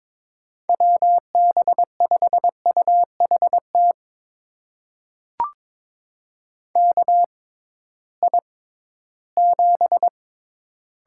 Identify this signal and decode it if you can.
Morse code, heard as a keyed tone, about 21 wpm, spaced as for 14 wpm, 700 Hz. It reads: WB5UHT KI7
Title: USA Call Signs - 6x Character Spacing - 22wpm